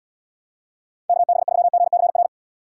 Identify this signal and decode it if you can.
Morse code, heard as a keyed tone, about 79 wpm, 700 Hz. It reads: KB3GCR